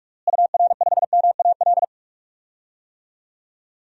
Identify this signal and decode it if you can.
Morse code, heard as a keyed tone, about 46 wpm, 700 Hz. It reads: UR5MAL
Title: DXCC-100 Call Signs - 6 Characters - Repeated 3x - 45wpm